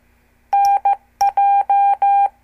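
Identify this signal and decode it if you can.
Morse code, heard as a keyed tone, about 15 wpm, 800 Hz. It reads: NJ
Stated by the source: Example CW QSO between N2APB and N2CX from a CWTD Morse lesson.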